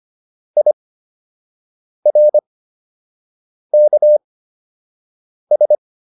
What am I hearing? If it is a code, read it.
Morse code, heard as a keyed tone, about 25 wpm, spaced as for 6 wpm, 600 Hz. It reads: IRKS
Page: fundamental-practice-files